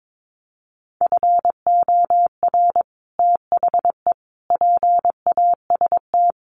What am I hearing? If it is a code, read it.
Morse code, heard as a keyed tone, about 22 wpm, 700 Hz. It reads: FOR THE PAST
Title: Sets of 5 Words Encore2 22wpm